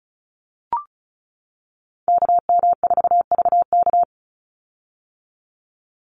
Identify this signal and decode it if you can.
Morse code, heard as a keyed tone, about 34 wpm, 700 Hz. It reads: KM4VK
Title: USA Call Signs 35wpm